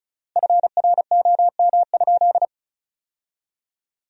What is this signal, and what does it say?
Morse code, heard as a keyed tone, about 36 wpm, 700 Hz. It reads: FROM?
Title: 3x Word-Spacing - Sentences from Top 300 Words - 35wpm